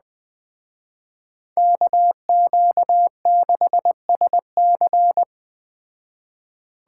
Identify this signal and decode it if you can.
Morse code, heard as a keyed tone, about 21 wpm, 700 Hz. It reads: KQ6SC